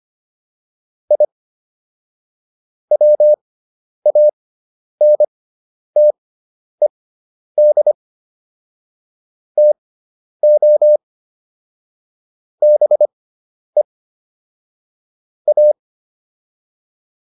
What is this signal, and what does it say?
Morse code, heard as a keyed tone, about 25 wpm, spaced as for 10 wpm, 600 Hz. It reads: I WANTED TO BE A